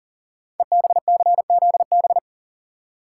Morse code, heard as a keyed tone, about 42 wpm, 700 Hz. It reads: EBCZB
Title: Code Groups - 5 Letters - 40wpm